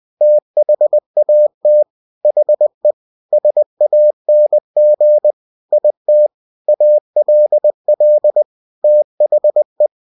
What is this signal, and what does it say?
Morse code, heard as a keyed tone, about 20 wpm, 600 Hz. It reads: THAT HE SANG IT ALL THE
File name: cw-books / winnie-the-pooh / winnie-20wpm-page-0076